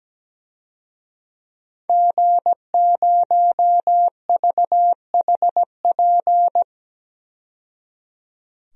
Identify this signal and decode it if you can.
Morse code, heard as a keyed tone, about 17 wpm, 700 Hz. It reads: G0VHP
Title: DXCC-100 Call Signs - 5 Characters - 17wpm